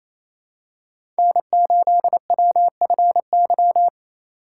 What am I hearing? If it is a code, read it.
Morse code, heard as a keyed tone, about 27 wpm, 700 Hz. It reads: N8WFY
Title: USA Call Signs - 5 Characters - 4x Character Spacing - 28wpm